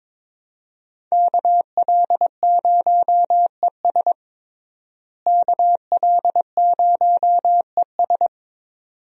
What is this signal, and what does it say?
Morse code, heard as a keyed tone, about 22 wpm, 700 Hz. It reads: KL0ES KL0ES
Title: USA Call Signs - 5 Characters - Repeated 2x - 22wpm